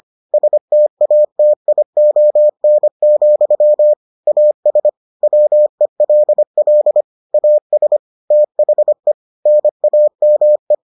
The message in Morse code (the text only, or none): STATION, AS WELL AS THE NAME